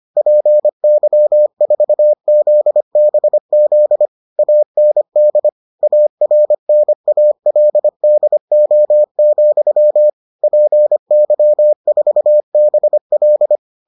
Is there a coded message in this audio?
Yes: Morse code from 0.2 to 13.6 s, heard as a keyed tone, about 25 wpm, 600 Hz. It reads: PY4ZBZ AND ARNALDO, PY4BL